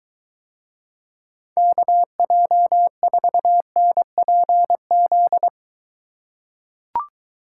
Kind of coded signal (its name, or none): Morse code